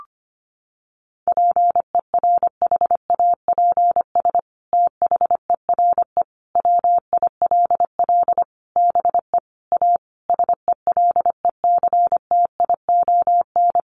Morse code, heard as a keyed tone, about 25 wpm, 700 Hz. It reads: PERHAPS THERE WILL BE A SELECTION
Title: Sentences from Top 1400 Words 25wpm